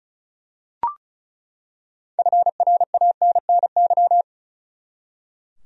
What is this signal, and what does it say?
Morse code, heard as a keyed tone, about 34 wpm, 700 Hz. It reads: FRANNY
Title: USA Common Names - 35wpm